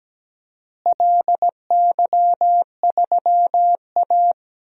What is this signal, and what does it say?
Morse code, heard as a keyed tone, about 17 wpm, 700 Hz. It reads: LY3A
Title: DXCC-100 Call Signs - 4 Characters - Repeated 2x - 17wpm